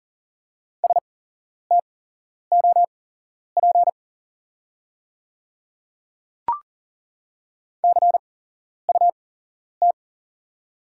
Morse code, heard as a keyed tone, about 40 wpm, spaced as for 11 wpm, 690 Hz. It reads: STOP CUT
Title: 8x Character Spacing - Top 300 Words - 40wpm